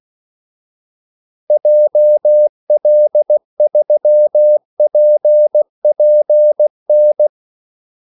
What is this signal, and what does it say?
Morse code, heard as a keyed tone, about 16 wpm, 600 Hz. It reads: JL3PPN
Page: Kaldesignaler 16 wpm